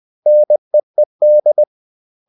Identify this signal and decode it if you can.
Morse code, heard as a keyed tone, about 20 wpm, 600 Hz. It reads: NEED
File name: CW / Practice / 500MostCommonEnglishWords20